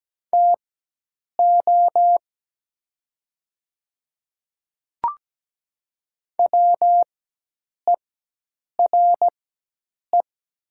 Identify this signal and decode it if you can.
Morse code, heard as a keyed tone, about 17 wpm, spaced as for 8 wpm, 700 Hz. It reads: TO WERE
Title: 4x Character Spacing - Top 200 Words - 17wpm